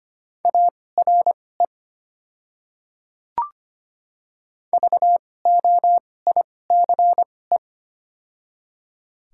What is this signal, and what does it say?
Morse code, heard as a keyed tone, about 25 wpm, spaced as for 18 wpm, 700 Hz. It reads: ARE VOICE
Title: 2x Character Spacing - Top 500 Words - 25wpm